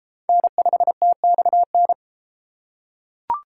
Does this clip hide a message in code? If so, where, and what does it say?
Yes: Morse code from 0.3 to 1.9 s, heard as a keyed tone, about 30 wpm, 700 Hz. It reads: N5TXN